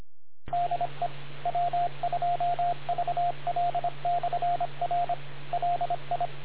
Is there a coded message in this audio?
Yes: Morse code from 0.5 to 6.3 s, heard as a keyed tone, about 26 wpm, 700 Hz, with radio noise. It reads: DE W2VL/R LI